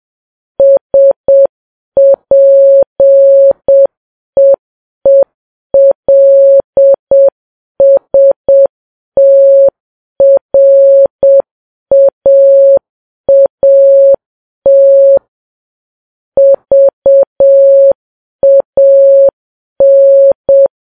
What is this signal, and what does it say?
Morse code, heard as a keyed tone, about 7 wpm, 550 Hz. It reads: SPEELSTRAAT VAN
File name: Files Web tls / Morse on9fv / morse7kps